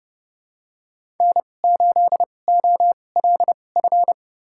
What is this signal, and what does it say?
Morse code, heard as a keyed tone, about 30 wpm, spaced as for 22 wpm, 700 Hz. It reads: N8OLF
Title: USA Call Signs - 5 Characters - 2x Character Spacing - 30wpm